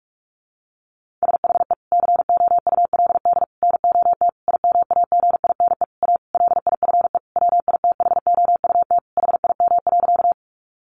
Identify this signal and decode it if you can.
Morse code, heard as a keyed tone, about 44 wpm, 700 Hz. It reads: SHE COULD NOT IMAGINE A LIFE WITHOUT HIM.
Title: Sentences from Top 1400 Words 45wpm